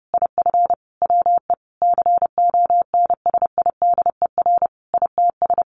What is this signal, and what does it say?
Morse code, heard as a keyed tone, about 30 wpm, 700 Hz. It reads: IF WE CONSIDER ITS
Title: Sentences from Top 1100 Words 30wpm